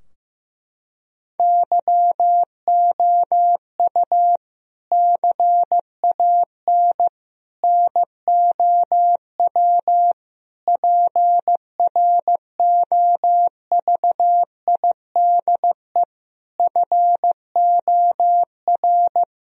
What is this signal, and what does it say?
Morse code, heard as a keyed tone, about 15 wpm, 700 Hz. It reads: YOU CAN NOW PROVIDE FOR
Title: Sentences from Top 1000 Words - Review Part 3 - 15wpm